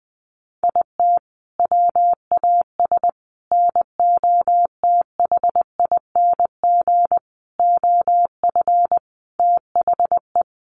Morse code, heard as a keyed tone, about 20 wpm, 700 Hz. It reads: IT WAS NOTHING OF THE
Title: Sentences from Top 1300 Words 20wpm